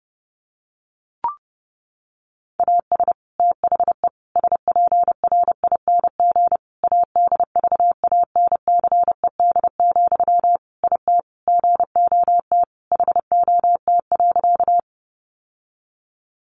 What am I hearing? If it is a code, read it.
Morse code, heard as a keyed tone, about 30 wpm, 700 Hz. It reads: AS THE SPRING ADVANCED, IT GOT HOT.
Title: Sentences from Top 1200 Words 30wpm